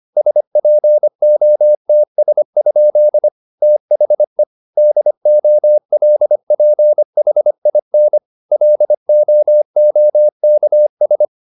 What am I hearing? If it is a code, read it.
Morse code, heard as a keyed tone, about 25 wpm, 600 Hz. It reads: SPOTS? THE DOLPHIN LOOKS